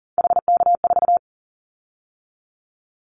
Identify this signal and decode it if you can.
Morse code, heard as a keyed tone, about 38 wpm, 700 Hz. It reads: HK4
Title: Callsign Prefixes for Top 100 DXCC - 40wpm